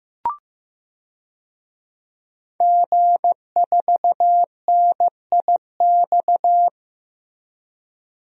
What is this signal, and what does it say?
Morse code, heard as a keyed tone, about 15 wpm, 700 Hz. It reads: G4NIX
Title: DXCC-100 Call Signs - 5 Characters - Repeated 3x - 15wpm